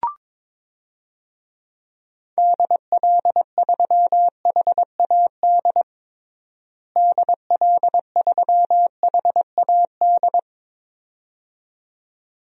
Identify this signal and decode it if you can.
Morse code, heard as a keyed tone, about 22 wpm, 700 Hz. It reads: DL3HAD DL3HAD
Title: DXCC-100 Call Signs - 6 Characters - Repeated 2x - 22wpm